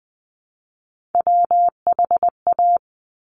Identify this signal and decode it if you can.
Morse code, heard as a keyed tone, about 20 wpm, 700 Hz. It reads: WHA
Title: Top Three Letter Combinations in Words 20wpm